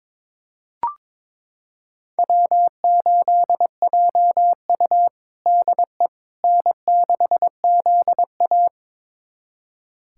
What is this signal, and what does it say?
Morse code, heard as a keyed tone, about 22 wpm, 700 Hz. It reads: W8JU DE N6ZA
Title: USA Callsign DE Callsign - 4 Characters - 22wpm